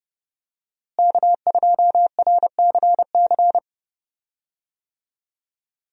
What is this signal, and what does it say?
Morse code, heard as a keyed tone, about 30 wpm, 700 Hz. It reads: K2RCC
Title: USA Call Signs - 5 Characters - Repeated 2x - 30wpm